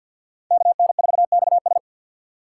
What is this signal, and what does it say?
Morse code, heard as a keyed tone, about 48 wpm, 690 Hz. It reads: KN4XS
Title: USA Call Signs - 5 Characters - 2x Character Spacing - 50wpm